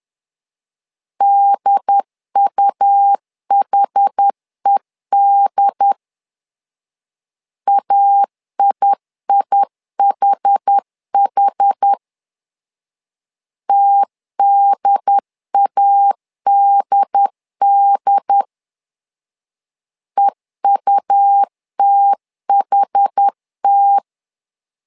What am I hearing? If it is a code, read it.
Morse code, heard as a keyed tone, about 10 wpm, 800 Hz. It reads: DUHED AIIHH TDADD EUTHT
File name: LEZIONI / 2 UHDN / 2 UHD 10wpm 1826